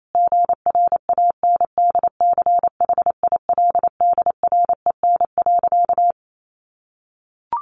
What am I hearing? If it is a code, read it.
Morse code, heard as a keyed tone, about 28 wpm, 700 Hz. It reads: GRANDCHILDREN.
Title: A Sound Of Thunder 28wpm